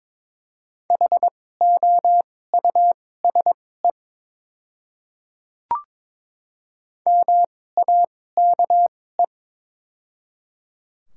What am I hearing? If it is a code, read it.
Morse code, heard as a keyed tone, about 21 wpm, spaced as for 16 wpm, 700 Hz. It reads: HOUSE MAKE
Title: 2x Character Spacing - Top 300 Words - 22wpm